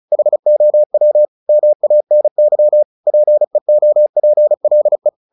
Morse code, heard as a keyed tone, about 36 wpm, 600 Hz. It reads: HOW MANY PEOPLE
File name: the old man and the sea 2 35wpm